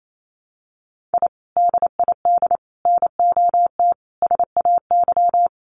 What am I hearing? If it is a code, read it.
Morse code, heard as a keyed tone, about 28 wpm, 700 Hz. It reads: I DID NOT SAY
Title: Sets of 5 Words Encore2 28wpm